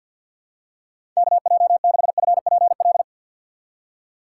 Morse code, heard as a keyed tone, about 47 wpm, 700 Hz. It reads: KJ6FPL
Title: USA Call Signs - 6 Characters - 3x Character Spacing - 50wpm